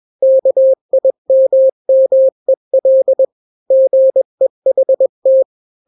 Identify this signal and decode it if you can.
Morse code, heard as a keyed tone, about 21 wpm, 530 Hz. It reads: KIMMEL GEHT
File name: daily / 20wpm